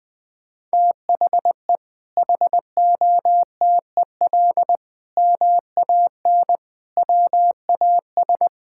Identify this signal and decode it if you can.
Morse code, heard as a keyed tone, about 20 wpm, 700 Hz. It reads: THE HOTEL MAN WAS
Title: Sentences from Top 1600 Words 20wpm